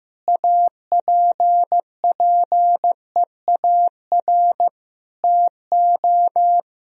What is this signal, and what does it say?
Morse code, heard as a keyed tone, about 15 wpm, 700 Hz. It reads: APPEAR TO
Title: Sentences from Top 1000 Words - Review Part 2 - 15wpm